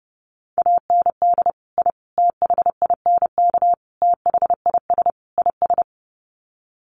Morse code, heard as a keyed tone, about 30 wpm, 700 Hz. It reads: AND I THINK THIS IS
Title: Sets of 5 Words Encore 30wpm